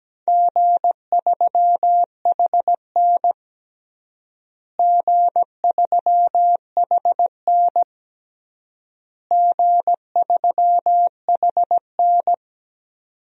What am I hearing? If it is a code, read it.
Morse code, heard as a keyed tone, about 17 wpm, 700 Hz. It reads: G3HN G3HN G3HN
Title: DXCC-100 Call Signs - 4 Characters - Repeated 3x - 17wpm